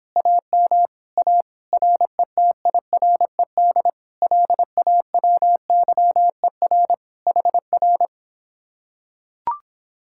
Morse code, heard as a keyed tone, about 26 wpm, 700 Hz. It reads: AM A RETIRED LAWYER HR